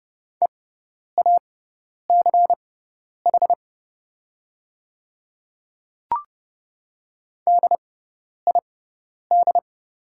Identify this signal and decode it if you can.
Morse code, heard as a keyed tone, about 30 wpm, spaced as for 10 wpm, 700 Hz. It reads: EACH DID
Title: 6x Character Spacing - Top 100 Words - 30wpm